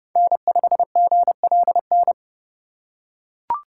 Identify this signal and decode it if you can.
Morse code, heard as a keyed tone, about 29 wpm, 700 Hz. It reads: N5GLN
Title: USA Call Signs - 5 Characters - 3x Character Spacing - 30wpm